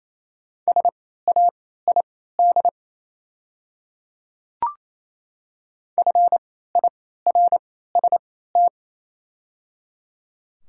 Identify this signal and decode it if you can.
Morse code, heard as a keyed tone, about 28 wpm, spaced as for 16 wpm, 700 Hz. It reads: SAID FIRST
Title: 3x Character Spacing - Top 100 Words - 28wpm